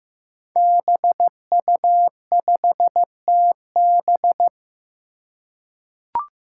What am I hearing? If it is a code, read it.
Morse code, heard as a keyed tone, about 15 wpm, 700 Hz. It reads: BU5TB
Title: Five Challenging Letter-Number Combinations 15wpm